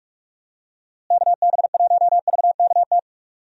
Morse code, heard as a keyed tone, about 46 wpm, 700 Hz. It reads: KB1VKT
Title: USA Call Signs - 6 Characters - Repeated 3x - 45wpm